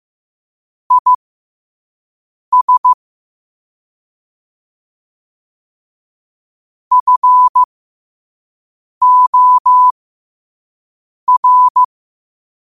Morse code, heard as a keyed tone, about 15 wpm, spaced as for 5 wpm, 1000 Hz. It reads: IS FOR
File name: qotdfiles / 5wpm